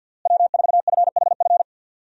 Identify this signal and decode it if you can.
Morse code, heard as a keyed tone, about 51 wpm, 700 Hz. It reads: W4FHF
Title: USA Call Signs - 5 Characters - Repeated 3x - 50wpm